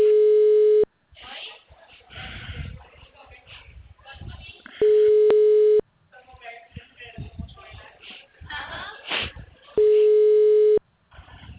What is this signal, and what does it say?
Ringback tone (Europe, 425 Hz), with speech.